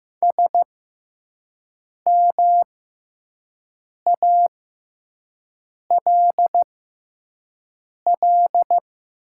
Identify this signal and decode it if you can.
Morse code, heard as a keyed tone, about 15 wpm, spaced as for 5 wpm, 700 Hz. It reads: SMALL
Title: 6x Character Spacing - Top 200 Words - 15wpm